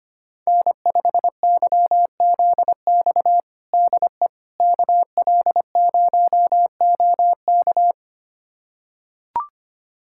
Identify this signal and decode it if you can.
Morse code, heard as a keyed tone, about 25 wpm, 700 Hz. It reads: N5YZX DE KL0OK
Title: USA Callsign DE Callsign - 5 Characters - 25wpm